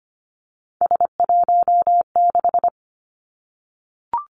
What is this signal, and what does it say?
Morse code, heard as a keyed tone, about 25 wpm, 700 Hz. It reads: S16
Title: Three Challenging Letter-Number Combinations 25wpm